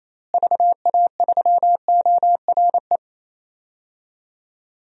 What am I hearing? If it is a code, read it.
Morse code, heard as a keyed tone, about 28 wpm, 700 Hz. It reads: VA3ORE
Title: DXCC-100 Call Signs - 6 Characters - Repeated 2x - 28wpm